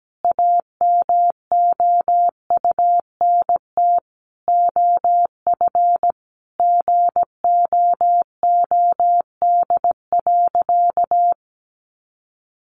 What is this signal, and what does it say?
Morse code, heard as a keyed tone, about 17 wpm, 700 Hz. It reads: AMOUNT OF GOOD.
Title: Sentences from Top 1300 Words 17wpm